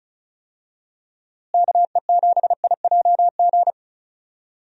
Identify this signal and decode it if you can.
Morse code, heard as a keyed tone, about 36 wpm, 700 Hz. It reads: KE7IJG
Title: USA Call Signs - 6 Characters - Repeated 2x - 35wpm